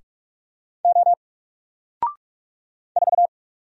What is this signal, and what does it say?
Morse code, heard as a keyed tone, about 44 wpm, 700 Hz. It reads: O 4
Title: Single Letter-Number 45wpm